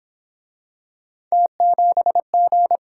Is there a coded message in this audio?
Yes: Morse code from 1.3 to 2.8 s, heard as a keyed tone, about 31 wpm, 700 Hz. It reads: T7G